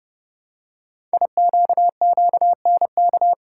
Morse code, heard as a keyed tone, about 30 wpm, 700 Hz. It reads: IQQNK
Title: Code Groups - 5 Letters - 30wpm